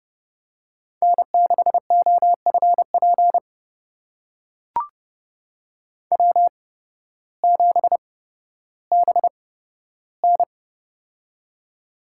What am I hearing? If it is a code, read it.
Morse code, heard as a keyed tone, about 30 wpm, spaced as for 14 wpm, 700 Hz. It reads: N6OFP W7BN
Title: USA Call Signs - 8x Character Spacing - 30wpm